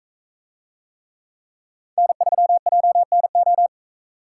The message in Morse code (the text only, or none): N3JNO